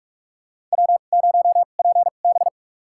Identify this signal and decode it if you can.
Morse code, heard as a keyed tone, about 46 wpm, spaced as for 32 wpm, 700 Hz. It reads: W0PB
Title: USA Call Signs - 4 Characters - 2x Character Spacing - 45wpm